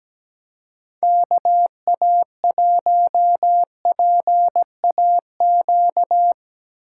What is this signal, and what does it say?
Morse code, heard as a keyed tone, about 18 wpm, 700 Hz. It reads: KA1PAQ
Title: Speed Racing - USA 2x3 Call Signs 17wpm